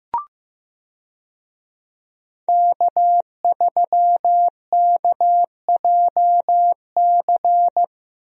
Morse code, heard as a keyed tone, about 15 wpm, 700 Hz. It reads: K3KJC
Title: USA Call Signs - 5 Characters - Repeated 2x - 15wpm